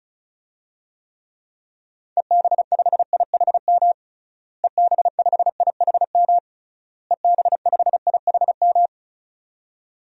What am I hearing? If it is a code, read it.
Morse code, heard as a keyed tone, about 36 wpm, 700 Hz. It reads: EB5IHM EB5IHM EB5IHM
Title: DXCC-100 Call Signs - 6 Characters - Repeated 3x - 35wpm